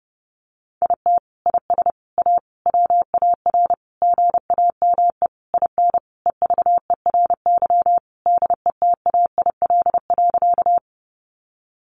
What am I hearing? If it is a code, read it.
Morse code, heard as a keyed tone, about 30 wpm, 700 Hz. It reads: IT IS A WAR GAME IN EVERY DETAIL.
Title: Sentences from Top 1500 Words 30wpm